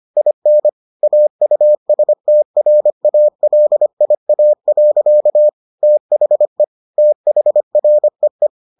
Morse code, heard as a keyed tone, about 25 wpm, 600 Hz. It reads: IN AUSTRALIA. THE THREE